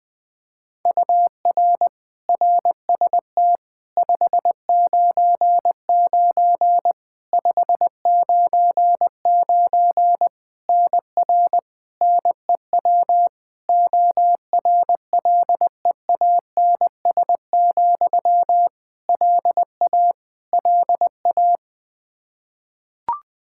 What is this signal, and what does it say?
Morse code, heard as a keyed tone, about 20 wpm, 700 Hz. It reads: UR RST 599 599 NR NEW ORLEANS, LA LA
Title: Practice QSO - Exchange 1 - Beginning - 20wpm